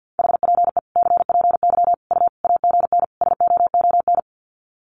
Morse code, heard as a keyed tone, about 50 wpm, 700 Hz. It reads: HPE CPY U AGN SOON
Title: Common QSO Phrases 50wpm